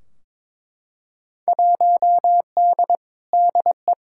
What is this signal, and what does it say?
Morse code, heard as a keyed tone, about 22 wpm, 700 Hz. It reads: 1D DE
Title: Field Day Exchanges - 22wpm